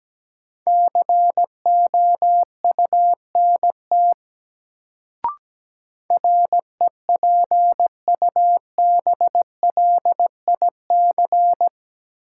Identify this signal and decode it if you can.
Morse code, heard as a keyed tone, about 17 wpm, 700 Hz. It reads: COUNT REPUBLIC
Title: Sentences from Top 1500 Words - New Word Drill Alt - 17wpm